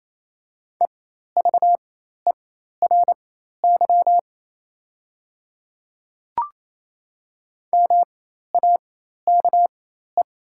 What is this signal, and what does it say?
Morse code, heard as a keyed tone, about 28 wpm, spaced as for 13 wpm, 700 Hz. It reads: EVERY MAKE
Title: 4x Character Spacing - Top 300 Words - 28wpm